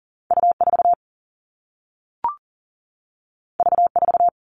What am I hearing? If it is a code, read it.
Morse code, heard as a keyed tone, about 40 wpm, 700 Hz. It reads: U4 V4
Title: All Cut Numbers - 2-Digit 40wpm